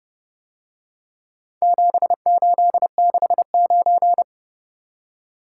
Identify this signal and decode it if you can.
Morse code, heard as a keyed tone, about 27 wpm, 700 Hz. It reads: 7869